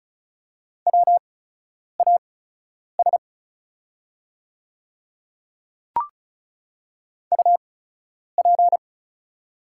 Morse code, heard as a keyed tone, about 33 wpm, spaced as for 9 wpm, 690 Hz. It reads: WAS UP
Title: 8x Character Spacing - Top 200 Words - 35wpm